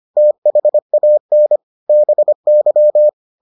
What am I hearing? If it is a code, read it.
Morse code, heard as a keyed tone, about 25 wpm, 600 Hz. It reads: THAN BY